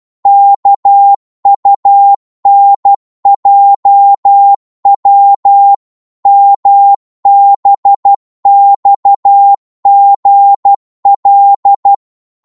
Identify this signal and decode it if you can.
Morse code, heard as a keyed tone, about 12 wpm, 800 Hz. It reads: KUNJW MBXGL
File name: Morse / ltr-12wpm